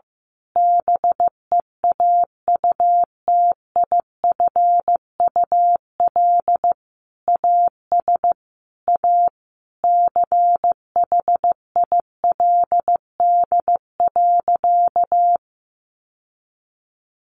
Morse code, heard as a keyed tone, about 15 wpm, 700 Hz. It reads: BEAUTIFUL AS A CHILD.
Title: Sentences from Top 1500 Words 15wpm